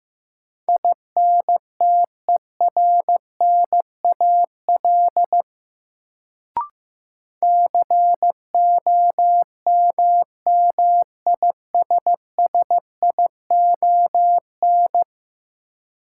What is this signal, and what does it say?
Morse code, heard as a keyed tone, about 16 wpm, 700 Hz. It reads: INTERNAL COMMISSION
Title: Sentences from Top 1400 Words - New Word Drill - 15wpm